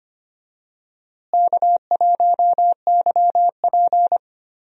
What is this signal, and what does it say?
Morse code, heard as a keyed tone, about 25 wpm, 700 Hz. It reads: K1YP
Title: USA Call Signs - 4 Characters - Repeated 3x - 25wpm